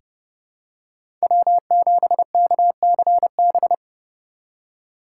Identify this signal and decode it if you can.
Morse code, heard as a keyed tone, about 30 wpm, 700 Hz. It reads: W7KCB
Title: USA Call Signs - 5 Characters - 8x Character Spacing - 30wpm